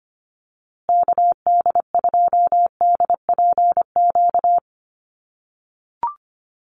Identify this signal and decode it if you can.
Morse code, heard as a keyed tone, about 25 wpm, 700 Hz. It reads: KD2DPQ
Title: USA Call Signs Repeated 2x 25wpm